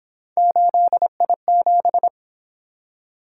Morse code, heard as a keyed tone, about 27 wpm, 700 Hz. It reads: 8I7
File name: Speed-Racing - Three Letter-Number Combinations - 22wpm